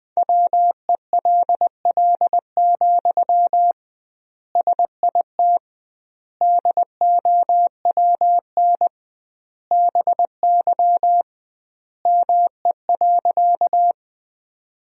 Morse code, heard as a keyed tone, about 20 wpm, 700 Hz. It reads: WELL, SIT DOWN BY ME.
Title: 2x Word-Spacing - Sentences from Top 500 Words - 20wpm